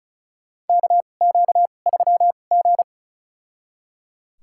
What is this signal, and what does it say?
Morse code, heard as a keyed tone, about 36 wpm, spaced as for 25 wpm, 700 Hz. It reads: KQ3G
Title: USA Call Signs - 4 Characters - 2x Character Spacing - 35wpm